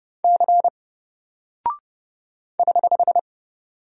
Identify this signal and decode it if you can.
Morse code, heard as a keyed tone, about 30 wpm, 700 Hz. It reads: C <HH>
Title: Single Character (Letter:Number:Punctuation:Prosign) Alt 30wpm